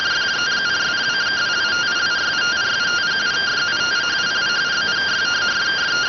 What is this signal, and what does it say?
Two-tone teleprinter signal, about 170 Hz shift.